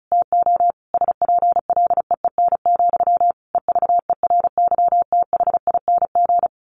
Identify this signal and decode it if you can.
Morse code, heard as a keyed tone, about 34 wpm, 700 Hz. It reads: TO SPLEEN, EVERYTHING
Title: A Sound Of Thunder 35wpm